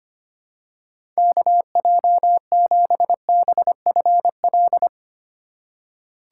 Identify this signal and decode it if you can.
Morse code, heard as a keyed tone, about 25 wpm, 700 Hz. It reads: KJ7BFL
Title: USA Call Signs - 6 Characters - 3x Character Spacing - 25wpm